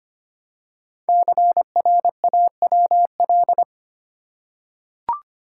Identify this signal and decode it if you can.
Morse code, heard as a keyed tone, about 25 wpm, 700 Hz. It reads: CRAWL
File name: 5 Letter Words - First 3 Letters Given - 25wpm